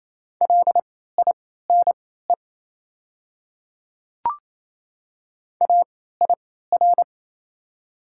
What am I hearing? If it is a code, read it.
Morse code, heard as a keyed tone, about 27 wpm, spaced as for 16 wpm, 700 Hz. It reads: LINE AIR
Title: 3x Character Spacing - Top 300 Words - 28wpm